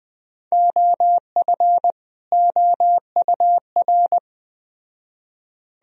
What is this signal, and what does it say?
Morse code, heard as a keyed tone, about 19 wpm, 700 Hz. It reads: OF OUR